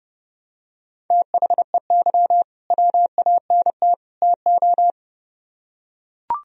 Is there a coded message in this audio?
Yes: Morse code from 1.1 to 4.9 s, heard as a keyed tone, about 30 wpm, 700 Hz. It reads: THEY WANT TO